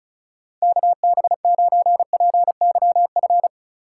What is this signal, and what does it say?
Morse code, heard as a keyed tone, about 34 wpm, 690 Hz. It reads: KB9PYF
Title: USA Call Signs - 6 Characters - 6x Character Spacing - 35wpm